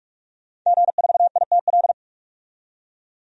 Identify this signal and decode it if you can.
Morse code, heard as a keyed tone, about 46 wpm, 700 Hz. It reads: G4ITL
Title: DXCC-100 Call Signs - 5 Characters - 45wpm